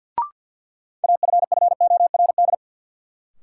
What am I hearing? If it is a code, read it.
Morse code, heard as a keyed tone, about 53 wpm, 700 Hz. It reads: AFFORD
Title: Sentences from Top 1100 Words - New Word Drill - 50wpm